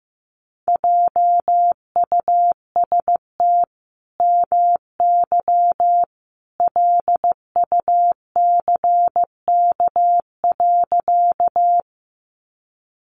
Morse code, heard as a keyed tone, about 15 wpm, 700 Hz. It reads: JUST MY LUCK.
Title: Sentences from Top 1500 Words 15wpm